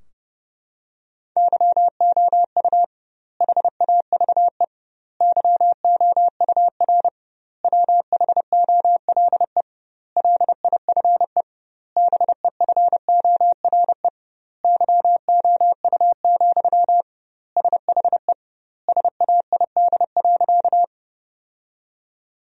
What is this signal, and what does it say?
Morse code, heard as a keyed tone, about 32 wpm, 700 Hz. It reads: YOU HAVE YOUR WHOLE LIFE BEFORE YOU, SHE SAID.
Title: 2x Word-Spacing - Sentences from Top 400 Words - 30wpm